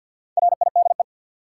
Morse code, heard as a keyed tone, about 54 wpm, 700 Hz. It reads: RIDE
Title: Sentences from Top 1200 Words - New Word Drill Alt - 50wpm